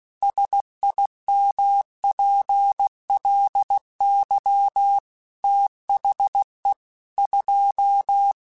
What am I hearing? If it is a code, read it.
Morse code, heard as a keyed tone, about 16 wpm, 800 Hz. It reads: SIMPLY THE 2
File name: Technical Subjects / Technical Subjects-mp3s / XC @15